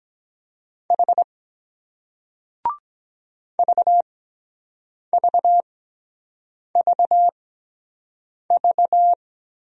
Morse code, heard as a keyed tone, about 23 wpm, 700 Hz. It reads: H V V V V